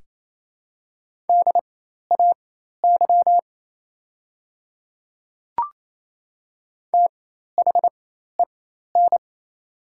Morse code, heard as a keyed tone, about 28 wpm, spaced as for 13 wpm, 700 Hz. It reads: DAY THEN
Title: 4x Character Spacing - Top 100 Words - 28wpm